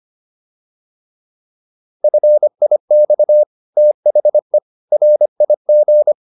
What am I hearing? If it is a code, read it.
Morse code, heard as a keyed tone, about 25 wpm, 600 Hz. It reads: FIX THE RIG